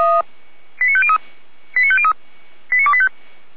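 Five-tone selective call (ZVEI), heard as quick runs of five tones, tones of about 70 ms. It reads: 89503 89503 89285
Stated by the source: Five-tone ZVEI selective call 89220, received by SDR, feuerwehr-orlamuende fire brigade.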